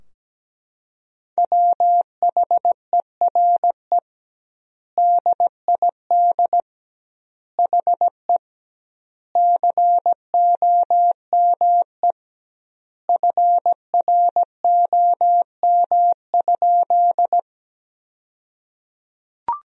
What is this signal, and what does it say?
Morse code, heard as a keyed tone, about 17 wpm, 700 Hz. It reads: WHERE DID HE COME FROM?
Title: 2x Word-Spacing - Sentences from Top 200 Words - 17wpm